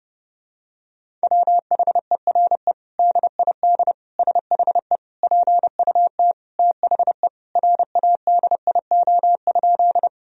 Morse code, heard as a keyed tone, about 32 wpm, 700 Hz. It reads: WHERE DID SHE PUT THE RADIO?
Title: Sentences from Top 1000 Words - Review Part 3 - 30wpm